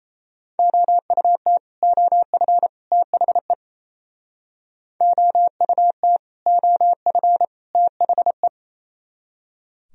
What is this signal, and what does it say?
Morse code, heard as a keyed tone, about 37 wpm, 700 Hz. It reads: OUT OF THE OUT OF THE